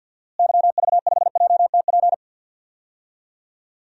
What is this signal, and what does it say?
Morse code, heard as a keyed tone, about 48 wpm, 700 Hz. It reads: YV5JTP
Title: DXCC-100 Call Signs - 6 Characters - Repeated 3x - 50wpm